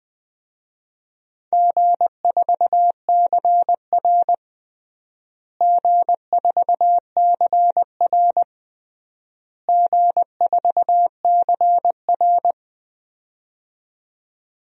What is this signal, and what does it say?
Morse code, heard as a keyed tone, about 20 wpm, 700 Hz. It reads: G4CR G4CR G4CR
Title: DXCC-100 Call Signs - 4 Characters - Repeated 3x - 20wpm